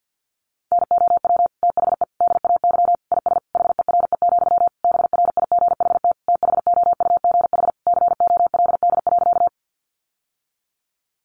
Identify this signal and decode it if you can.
Morse code, heard as a keyed tone, about 50 wpm, 700 Hz. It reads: NOW THE DAY IS HERE, BRIGHT THOUGH COLD.
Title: Sentences from Top 1000 Words 50wpm